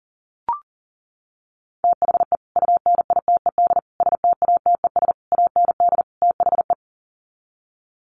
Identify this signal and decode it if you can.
Morse code, heard as a keyed tone, about 40 wpm, 700 Hz. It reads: THE UNITED STATES AND THE
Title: Sets of 5 Words 40wpm